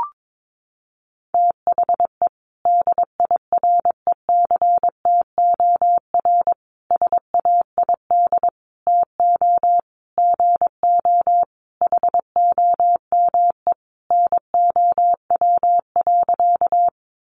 Morse code, heard as a keyed tone, about 22 wpm, 700 Hz. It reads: THE DIRECTOR SAID TO GO HOME NOW.
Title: Sentences from Top 1300 Words 22wpm